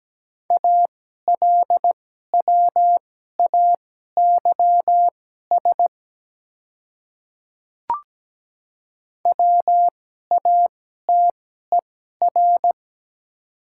Morse code, heard as a keyed tone, about 17 wpm, spaced as for 12 wpm, 690 Hz. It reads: ALWAYS WATER
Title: 2x Character Spacing - Top 300 Words - 17wpm